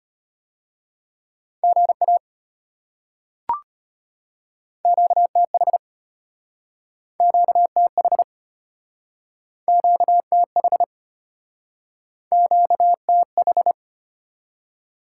Morse code, heard as a keyed tone, about 34 wpm, 700 Hz. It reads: GA QTH QTH QTH QTH